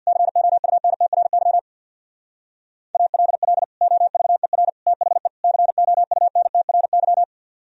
Morse code, heard as a keyed tone, about 62 wpm, 700 Hz. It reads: COUNTRY ALL OVER THE COUNTRY
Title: Speed-Racing - Sets of 4 Words 50wpm